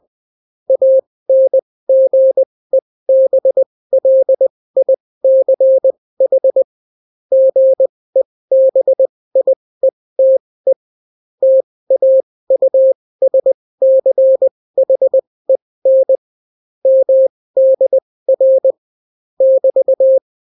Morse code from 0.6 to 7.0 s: ANGEBLICH